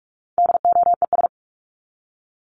Morse code, heard as a keyed tone, about 44 wpm, 700 Hz. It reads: DOES
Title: Top 500 Words 45wpm